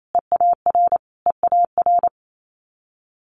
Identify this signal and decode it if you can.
Morse code, heard as a keyed tone, about 28 wpm, 700 Hz. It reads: EAR EAR
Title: Multi-letter Phonemes Repeated 3x 28wpm